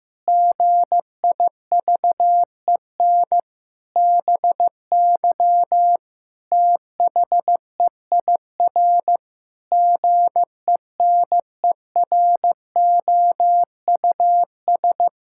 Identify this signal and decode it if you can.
Morse code, heard as a keyed tone, about 15 wpm, 700 Hz. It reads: GIVEN BY THEIR GENEROUS
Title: Pangram Challenge v2 - 15wpm